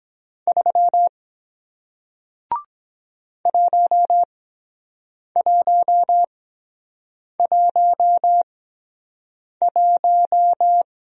Morse code, heard as a keyed tone, about 23 wpm, spaced as for 7 wpm, 700 Hz. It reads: 3 1111